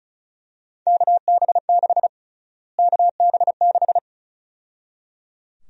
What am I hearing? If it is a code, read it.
Morse code, heard as a keyed tone, about 34 wpm, 700 Hz. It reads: KB6 KB6
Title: USA Call Sign Prefixes - Repeated 2x - 35wpm